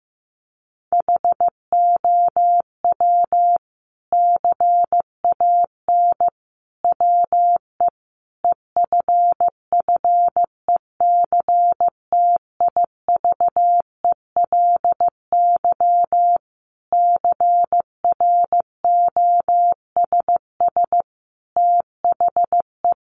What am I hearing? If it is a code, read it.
Morse code, heard as a keyed tone, about 15 wpm, 700 Hz. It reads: HOW CAN WE EFFECTIVELY CROSS THE
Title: Sentences from Top 1400 Words 15wpm